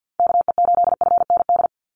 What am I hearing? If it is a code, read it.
Morse code, heard as a keyed tone, about 50 wpm, 700 Hz. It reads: KE8FND
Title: USA Call Signs 50wpm